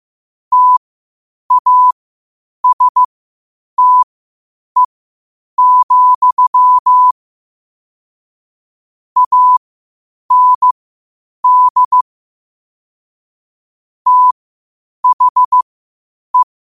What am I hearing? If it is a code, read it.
Morse code, heard as a keyed tone, about 15 wpm, spaced as for 8 wpm, 1000 Hz. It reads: TASTE, AND THE
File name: qotdfiles / 8wpm